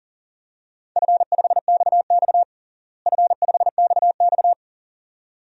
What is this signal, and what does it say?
Morse code, heard as a keyed tone, about 40 wpm, 700 Hz. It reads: F5XX F5XX
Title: DXCC-100 Call Signs - 4 Characters - Repeated 2x - 40wpm